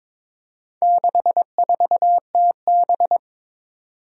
Morse code, heard as a keyed tone, about 21 wpm, 700 Hz. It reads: 64TB
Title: Four Challenging Letter-Number Combinations 22wpm